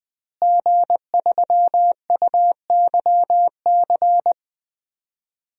Morse code, heard as a keyed tone, about 21 wpm, 700 Hz. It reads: G3UYC
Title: DXCC-100 Call Signs - 5 Characters - Repeated 3x - 20wpm